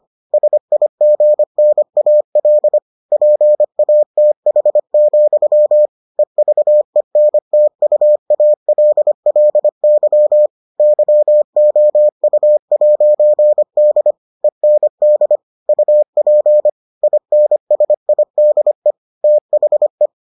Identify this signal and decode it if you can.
Morse code, heard as a keyed tone, about 25 wpm, 600 Hz. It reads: SIGNAL PATH, EVENTUALLY YOU'D END UP INSIDE THE